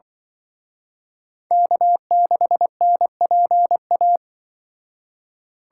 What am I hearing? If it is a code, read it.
Morse code, heard as a keyed tone, about 24 wpm, 700 Hz. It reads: K6NPA
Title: Speed Racing - USA 1x3 Call Signs 20wpm